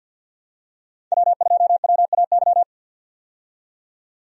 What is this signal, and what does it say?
Morse code, heard as a keyed tone, about 49 wpm, 700 Hz. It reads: W2WAY
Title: USA Call Signs - 5 Characters - 50wpm